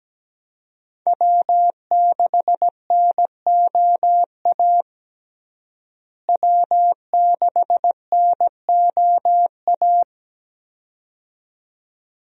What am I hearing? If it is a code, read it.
Morse code, heard as a keyed tone, about 17 wpm, 700 Hz. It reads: W6NOA W6NOA
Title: USA Call Signs - 5 Characters - Repeated 2x - 17wpm